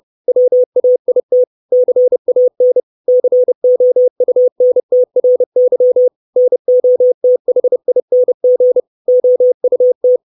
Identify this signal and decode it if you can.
Morse code, heard as a keyed tone, about 30 wpm, 500 Hz. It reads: WAIT CAN COUNTRY NOTHING OUT
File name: 632 Words 30WPM